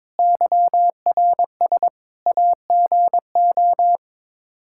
Morse code, heard as a keyed tone, about 21 wpm, 700 Hz. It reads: YRS AGO
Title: K1USN SST QSO Practice 22wpm